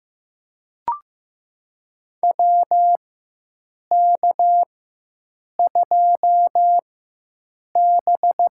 Morse code, heard as a keyed tone, about 15 wpm, spaced as for 7 wpm, 700 Hz. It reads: WK2B
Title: USA Call Signs - 4 Characters - 4x Character Spacing - 15wpm